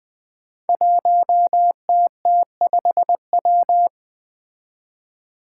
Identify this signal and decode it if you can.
Morse code, heard as a keyed tone, about 20 wpm, 700 Hz. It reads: 1TT5W
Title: Five Challenging Letter-Number Combinations 20wpm